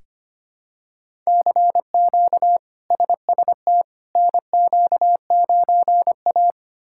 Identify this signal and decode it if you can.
Morse code, heard as a keyed tone, about 25 wpm, 700 Hz. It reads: CQ SST NQ9A
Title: K1USN SST QSO Practice - 25wpm